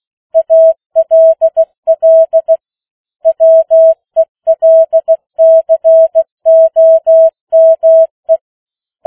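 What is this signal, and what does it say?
Morse code, heard as a keyed tone, about 16 wpm, 650 Hz. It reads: ALL WELCOME
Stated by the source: Opening of the New Zealand national amateur radio net at 15 wpm, which meets on 3535.0 kHz.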